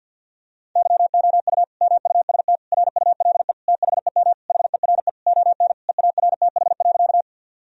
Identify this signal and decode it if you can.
Morse code, heard as a keyed tone, about 50 wpm, 700 Hz. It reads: YOU MUST RULE THEM HERE ON EARTH.
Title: Sentences from Top 1000 Words - Review Part 2 - 50wpm